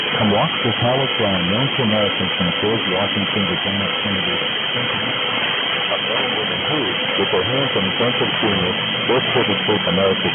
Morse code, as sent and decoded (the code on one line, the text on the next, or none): none